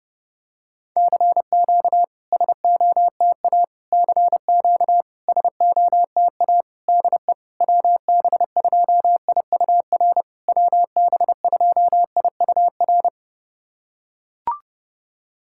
Morse code, heard as a keyed tone, about 30 wpm, 690 Hz. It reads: CQ SOTA CQ SOTA DE WB2IUR WB2IUR
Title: SOTA QSO Practice 30wpm